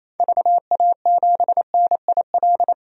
Morse code, heard as a keyed tone, about 28 wpm, 700 Hz. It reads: VA7NIL
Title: DXCC-100 Call Signs - 6 Characters - 28wpm